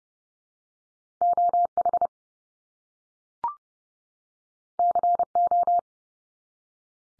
Morse code, heard as a keyed tone, about 30 wpm, 700 Hz. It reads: OH CO
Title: US States 30WPM